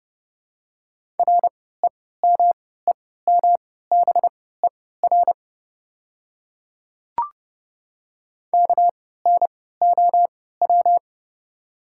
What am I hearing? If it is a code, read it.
Morse code, heard as a keyed tone, about 32 wpm, spaced as for 17 wpm, 700 Hz. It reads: REMEMBER KNOW
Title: 3x Character Spacing - Top 500 Words - 30wpm